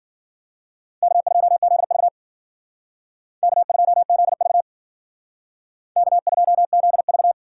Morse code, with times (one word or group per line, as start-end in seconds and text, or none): 1.0-2.1 K2ZV
3.4-4.6 K2ZV
6.0-7.3 K2ZV